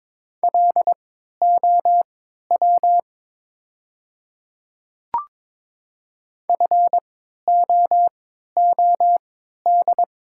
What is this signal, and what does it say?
Morse code, heard as a keyed tone, about 23 wpm, spaced as for 13 wpm, 700 Hz. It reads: LOW FOOD
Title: 3x Character Spacing - Top 300 Words - 22wpm